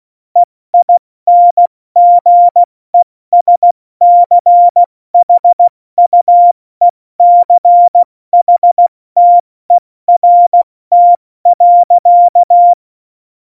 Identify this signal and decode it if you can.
Morse code, heard as a keyed tone, about 16 wpm, 700 Hz. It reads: EINGESCHUECHTERT.
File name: Morsekurs / Witze / Witz 13. November 2018, 70 F80